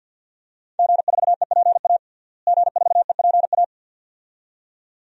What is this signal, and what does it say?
Morse code, heard as a keyed tone, about 51 wpm, 700 Hz. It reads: G4EPA G4EPA
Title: DXCC-100 Call Signs - 5 Characters - Repeated 2x - 50wpm